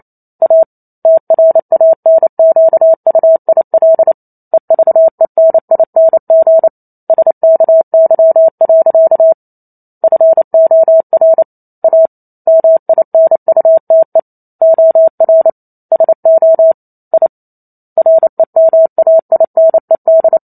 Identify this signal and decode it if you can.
Morse code, heard as a keyed tone, about 28 wpm, 650 Hz. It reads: A TRANQUIL EVENING SKY. FOR A MINUTE OR SO I REMAINED